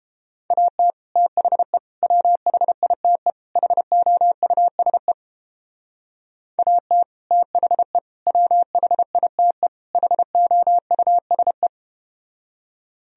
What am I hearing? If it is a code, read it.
Morse code, heard as a keyed tone, about 32 wpm, 700 Hz. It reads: AT THE WHITE HOUSE AT THE WHITE HOUSE